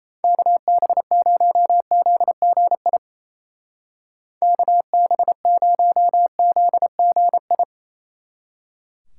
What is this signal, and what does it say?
Morse code, heard as a keyed tone, about 35 wpm, 700 Hz. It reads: KB0ZGI KB0ZGI
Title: Speed Racing - USA 2x3 Call Signs 28wpm